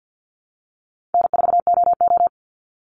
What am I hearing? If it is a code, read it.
Morse code, heard as a keyed tone, about 49 wpm, 700 Hz. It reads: N4OO
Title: Call Signs Repeated 3x - 50wpm